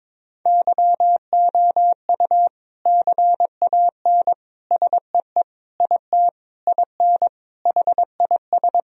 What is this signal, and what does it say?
Morse code, heard as a keyed tone, about 22 wpm, 700 Hz. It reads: YOU CAN SEE IT IN HIS
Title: Sentences from Top 1000 Words - Review Part 1 - 22wpm